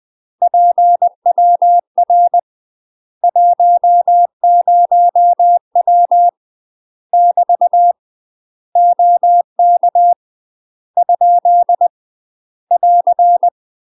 Morse code, heard as a keyed tone, about 20 wpm, 700 Hz. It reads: PWR 10W <BT> OK ? <AR>
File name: UK MASTER QSOs TRAINING / 20WPM / UK MASTER QSOs 0051